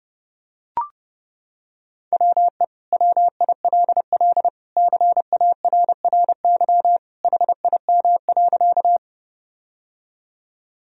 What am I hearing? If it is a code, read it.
Morse code, heard as a keyed tone, about 30 wpm, 700 Hz. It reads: WE WILL CARRY HIM.
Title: Sentences from Top 1000 Words - Review Part 1 - 30wpm